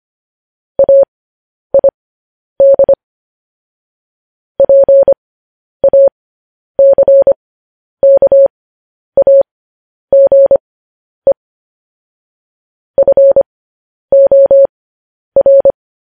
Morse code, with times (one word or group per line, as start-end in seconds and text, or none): 0.8-2.9 AID
4.6-11.3 PACKAGE
13.0-15.7 FOR